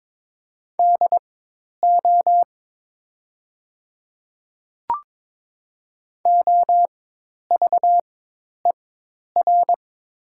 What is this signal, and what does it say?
Morse code, heard as a keyed tone, about 22 wpm, spaced as for 10 wpm, 700 Hz. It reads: DO OVER
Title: 4x Character Spacing - Top 100 Words - 22wpm